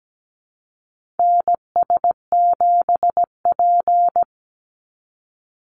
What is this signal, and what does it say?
Morse code, heard as a keyed tone, about 17 wpm, 700 Hz. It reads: NS7P
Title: NA SOTA Chaser Callsigns 17wpm